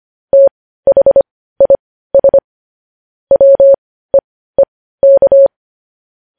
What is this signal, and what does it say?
Morse code, heard as a keyed tone, about 25 wpm, spaced as for 15 wpm, 550 Hz. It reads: THIS WEEK
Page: news-headlines-in-morse-code-at-15-wpm